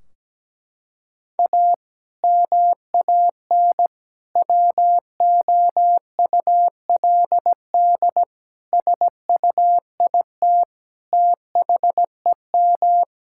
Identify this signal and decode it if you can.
Morse code, heard as a keyed tone, about 17 wpm, 700 Hz. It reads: A MAN WOULD SUIT THEM
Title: Sentences from Top 1600 Words 17wpm